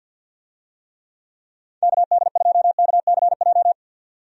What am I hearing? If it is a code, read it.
Morse code, heard as a keyed tone, about 52 wpm, 700 Hz. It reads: KD2KCJ
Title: USA Call Signs - 6 Characters - 50wpm